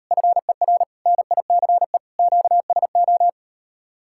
Morse code, heard as a keyed tone, about 39 wpm, 700 Hz. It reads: FER NICE QSO